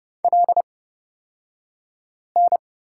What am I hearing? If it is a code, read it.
Morse code, heard as a keyed tone, about 29 wpm, spaced as for 5 wpm, 700 Hz. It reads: LN
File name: Lesson 36d - BT - Two Characters 30wpm